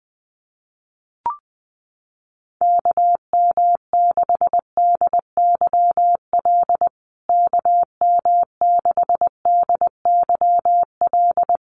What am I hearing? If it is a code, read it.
Morse code, heard as a keyed tone, about 19 wpm, 700 Hz. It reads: KM6DYL KM6DYL
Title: USA Call Signs Repeated 3x 20wpm